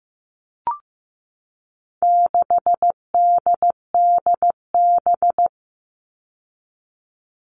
Morse code, heard as a keyed tone, about 15 wpm, 700 Hz. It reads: 6DDB
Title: All Cut Numbers - 4-Digit 15wpm